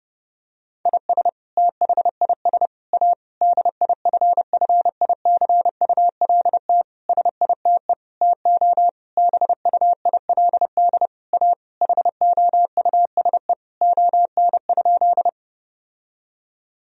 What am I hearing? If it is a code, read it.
Morse code, heard as a keyed tone, about 32 wpm, 700 Hz. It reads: IS THIS A DIFFICULT SITE TO BUILD A HOUSE ON?
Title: Sentences from Top 1000 Words - Review Part 4 - 30wpm